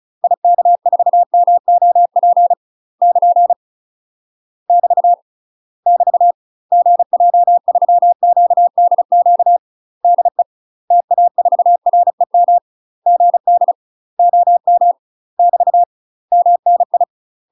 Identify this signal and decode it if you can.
Morse code, heard as a keyed tone, about 35 wpm, 700 Hz. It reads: IK4MOP <KN> <BT> <BT> GJ3QDQ DE TA4REM GD OM <BT> MNI
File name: UK MASTER QSOs TRAINING / 35WPM / UK MASTER QSOs 0043